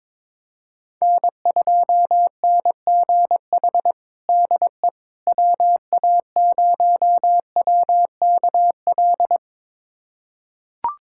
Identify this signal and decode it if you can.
Morse code, heard as a keyed tone, about 21 wpm, 700 Hz. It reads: N2NGH DE WA0WKL
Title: USA Callsign DE Callsign 22wpm